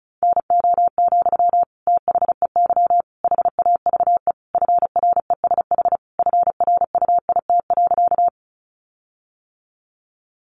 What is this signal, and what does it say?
Morse code, heard as a keyed tone, about 34 wpm, 700 Hz. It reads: NO, THEY HAVE FRESH FRUIT.
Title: Sentences from Top 1300 Words 35wpm